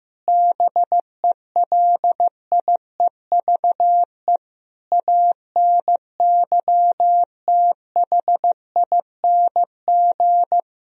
Morse code, heard as a keyed tone, about 16 wpm, 700 Hz. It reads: BELIEVE ANYTHING
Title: Sentences from Top 1000 Words - Review Part 1 - 15wpm